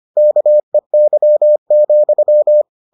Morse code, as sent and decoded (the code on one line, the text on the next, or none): -.- . -.-- --..--
KEY,